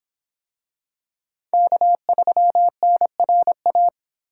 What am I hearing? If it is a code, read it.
Morse code, heard as a keyed tone, about 26 wpm, 700 Hz. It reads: K3NRA